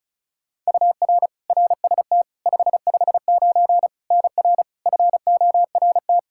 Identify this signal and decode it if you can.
Morse code, heard as a keyed tone, about 36 wpm, 700 Hz. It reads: UR RST 559 NR FORT
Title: Practice QSO - Exchange 1 - Beginning - 35wpm